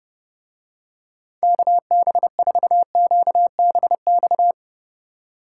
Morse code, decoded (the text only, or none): KB4QBX